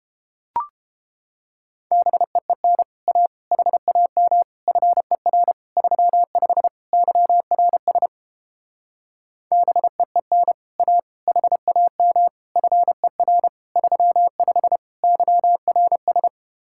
Morse code, heard as a keyed tone, about 29 wpm, 700 Hz. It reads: BEEN A HAM FER 35 YRS BEEN A HAM FER 35 YRS